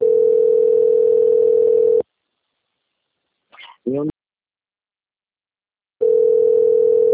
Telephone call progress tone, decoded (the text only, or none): ringback tone